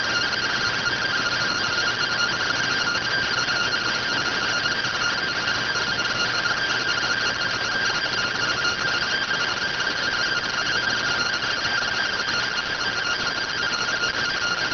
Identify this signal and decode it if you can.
Two-tone teleprinter signal, about 170 Hz shift.